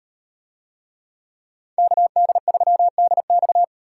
Morse code, heard as a keyed tone, about 38 wpm, 700 Hz. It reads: KD3DX